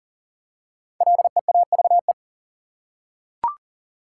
Morse code, heard as a keyed tone, about 40 wpm, 700 Hz. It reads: LEAVE
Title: Five Letter Words - 40wpm